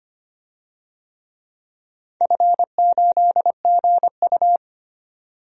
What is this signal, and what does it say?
Morse code, heard as a keyed tone, about 25 wpm, 700 Hz. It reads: F8GU
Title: DXCC-100 Call Signs - 4 Characters - 25wpm